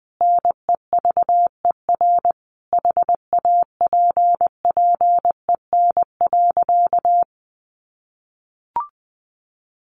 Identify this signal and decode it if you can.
Morse code, heard as a keyed tone, about 20 wpm, 700 Hz. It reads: NEVER HAPPEN.
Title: Sentences from Top 1200 Words 20wpm